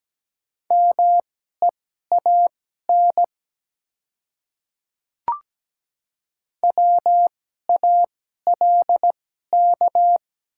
Morse code, heard as a keyed tone, about 17 wpm, spaced as for 12 wpm, 700 Hz. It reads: MEAN WALK
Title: 2x Character Spacing - Top 300 Words - 17wpm